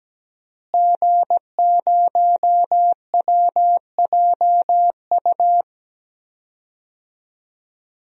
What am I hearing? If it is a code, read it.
Morse code, heard as a keyed tone, about 17 wpm, 700 Hz. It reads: G0WJU
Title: DXCC-100 Call Signs - 5 Characters - Repeated 3x - 17wpm